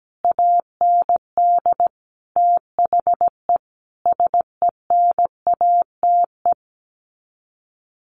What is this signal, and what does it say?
Morse code, heard as a keyed tone, about 17 wpm, 700 Hz. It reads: AND THE SENATE
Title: Sets of 5 Words Encore2 17wpm